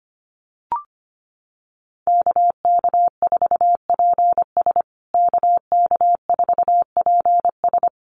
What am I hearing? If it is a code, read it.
Morse code, heard as a keyed tone, about 25 wpm, 700 Hz. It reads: KK4PS KK4PS
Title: USA Call Signs Repeated 3x 25wpm